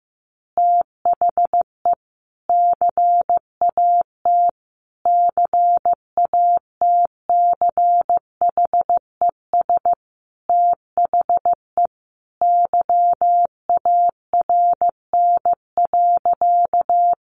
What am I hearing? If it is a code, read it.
Morse code, heard as a keyed tone, about 15 wpm, 700 Hz. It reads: THE CAT CATCHES THE YARN.
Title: Additional Easy Phrases 15wpm